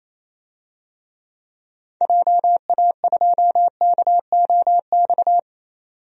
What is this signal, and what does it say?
Morse code, heard as a keyed tone, about 28 wpm, 700 Hz. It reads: JA2KOX
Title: DXCC-100 Call Signs - 6 Characters - 28wpm